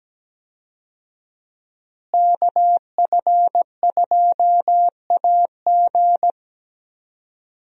Morse code, heard as a keyed tone, about 17 wpm, 700 Hz. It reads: KF2AG
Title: USA Call Signs - 5 Characters - Repeated 3x - 17wpm